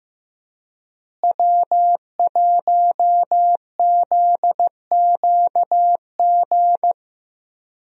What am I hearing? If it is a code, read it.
Morse code, heard as a keyed tone, about 16 wpm, 700 Hz. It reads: W1ZQG
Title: USA Call Signs - 5 Characters - Repeated 3x - 15wpm